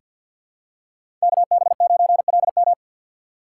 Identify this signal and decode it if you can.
Morse code, heard as a keyed tone, about 50 wpm, 700 Hz. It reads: KB9LM
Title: USA Call Signs - 5 Characters - 2x Character Spacing - 50wpm